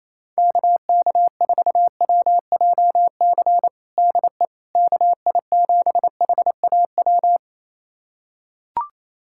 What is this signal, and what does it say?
Morse code, heard as a keyed tone, about 28 wpm, 700 Hz. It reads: KK4WJC DE KI7HAW
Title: USA Callsign DE Callsign - 6 Characters - 28wpm